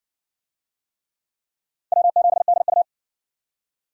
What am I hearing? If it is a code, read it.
Morse code, heard as a keyed tone, about 60 wpm, 700 Hz. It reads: W7DU